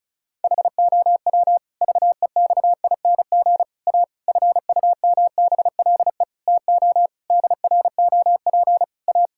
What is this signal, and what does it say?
Morse code, heard as a keyed tone, about 34 wpm, 700 Hz. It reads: HOW VEXING A FUMBLE TO DROP A
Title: Pangram Challenge v2 - 35wpm